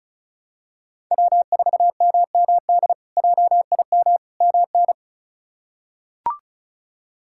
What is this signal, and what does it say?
Morse code, heard as a keyed tone, about 36 wpm, 700 Hz. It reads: W4MMD JIM MN
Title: K1USN SST QSO Practice - 35wpm